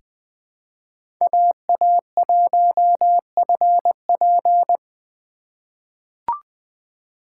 Morse code, heard as a keyed tone, about 20 wpm, 700 Hz. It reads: AA1FP
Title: USA Call Signs - 4x Character Spacing - 20wpm